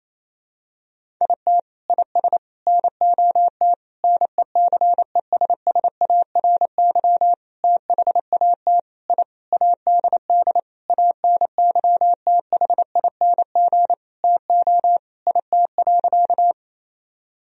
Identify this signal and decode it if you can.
Morse code, heard as a keyed tone, about 28 wpm, 700 Hz. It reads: IT IS NOT NECESSARY THAT I ADD ANYTHING TO IT.
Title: Sentences from Top 1000 Words - Review Part 1 - 28wpm